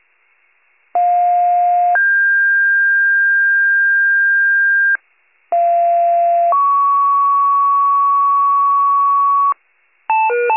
Two-tone page, heard as a one-second tone, then a three-second tone, about 710 then 1600 Hz, about 690 then 1120 Hz, with noise. It is heard from 1.0 to 9.5 s.